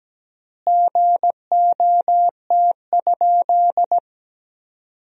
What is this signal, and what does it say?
Morse code, heard as a keyed tone, about 17 wpm, 700 Hz. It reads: GOT?
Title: 3x Word-Spacing - Sentences from Top 300 Words - 17wpm